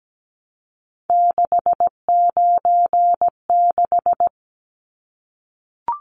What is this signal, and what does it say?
Morse code, heard as a keyed tone, about 17 wpm, 700 Hz. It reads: 696
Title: Numbers - 3-Digit 17wpm